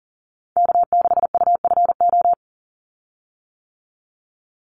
Morse code, heard as a keyed tone, about 40 wpm, 700 Hz. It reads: K6UFO
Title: Call Signs - 40wpm